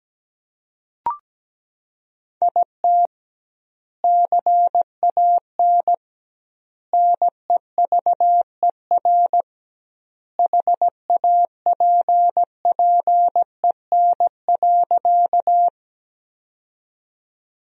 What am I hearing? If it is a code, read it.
Morse code, heard as a keyed tone, about 17 wpm, 700 Hz. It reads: IT CAN NEVER HAPPEN.
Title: 2x Word-Spacing - Sentences from Top 400 Words - 17wpm